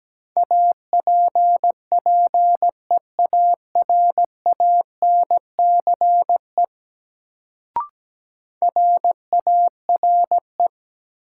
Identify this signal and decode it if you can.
Morse code, heard as a keyed tone, about 17 wpm, 700 Hz. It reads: APPEARANCE RARE
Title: Sentences from Top 1100 Words - New Word Drill Alt - 17wpm